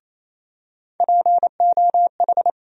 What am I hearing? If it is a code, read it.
Morse code, heard as a keyed tone, about 28 wpm, 700 Hz. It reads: POH
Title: USA Call Sign Postfixes - 3 Letters 28wpm